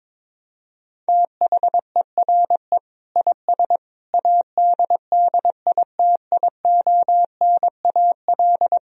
Morse code, heard as a keyed tone, about 22 wpm, 700 Hz. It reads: THERE IS ADDITIONAL
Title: Sentences from Top 1000 Words - Review Part 4 - 22wpm